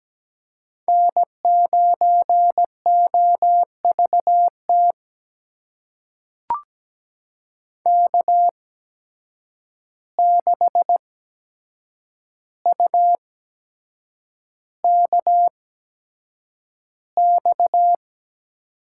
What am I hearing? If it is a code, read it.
Morse code, heard as a keyed tone, about 17 wpm, spaced as for 7 wpm, 700 Hz. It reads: N9OVT K6UKX
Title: USA Call Signs - 5 Characters - 8x Character Spacing - 17wpm